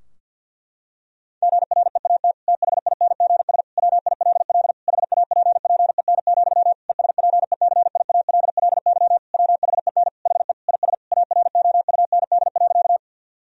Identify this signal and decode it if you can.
Morse code, heard as a keyed tone, about 50 wpm, 700 Hz. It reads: GREAT THINGS WILL HAPPEN, ESPECIALLY WHEN HE IS AROUND.
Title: Sentences from Top 1000 Words - Review Part 4 - 50wpm